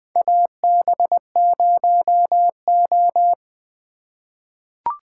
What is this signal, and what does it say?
Morse code, heard as a keyed tone, about 20 wpm, 700 Hz. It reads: AB0O
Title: 2023 POTA Activator Call Signs - 20wpm